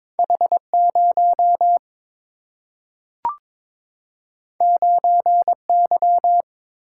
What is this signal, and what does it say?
Morse code, heard as a keyed tone, about 22 wpm, 700 Hz. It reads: H0 9Y
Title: Single-Letter Single-Number 22wpm